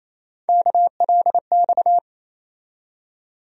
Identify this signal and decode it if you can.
Morse code, heard as a keyed tone, about 28 wpm, 700 Hz. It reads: KLX
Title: USA Call Sign Postfixes - 3 Letters 28wpm